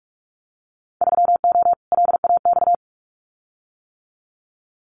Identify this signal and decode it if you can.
Morse code, heard as a keyed tone, about 46 wpm, 690 Hz. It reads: 3O LAX
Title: Winter Field Day Exchanges Encore 45wpm